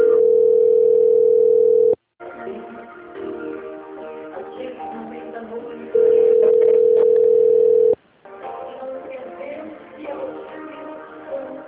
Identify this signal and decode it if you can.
Ringback tone (North America), 440 + 480 Hz.
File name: out- - - - - .39766